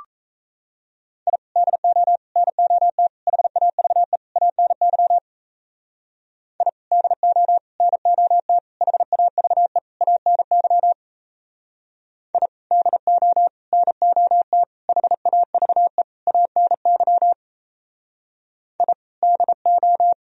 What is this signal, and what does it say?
Morse code, heard as a keyed tone, about 37 wpm, 700 Hz. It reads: I DO NOT HAVE ANY I DO NOT HAVE ANY I DO NOT HAVE ANY I DO